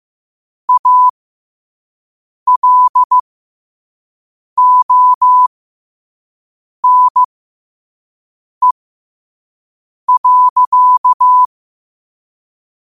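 Morse code, heard as a keyed tone, about 15 wpm, spaced as for 5 wpm, 1000 Hz. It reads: ALONE.
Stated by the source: Morse code quote of the day at 5 wpm.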